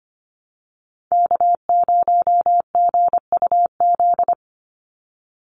Morse code, heard as a keyed tone, about 25 wpm, 700 Hz. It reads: K0GUZ
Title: Call Signs Repeated 2x - 25wpm